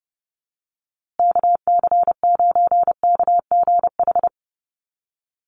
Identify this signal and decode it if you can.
Morse code, heard as a keyed tone, about 30 wpm, 700 Hz. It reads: KC9KGH
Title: USA Call Signs Repeated 3x 30wpm